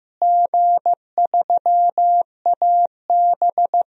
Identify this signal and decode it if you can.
Morse code, heard as a keyed tone, about 15 wpm, 700 Hz. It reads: G3AB
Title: DXCC-100 Call Signs - 4 Characters - Repeated 3x - 15wpm